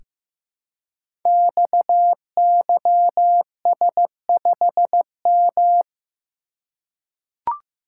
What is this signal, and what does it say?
Morse code, heard as a keyed tone, about 15 wpm, 700 Hz. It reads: XYS5M
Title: Code Groups - 5 Letter:Numbers - 15wpm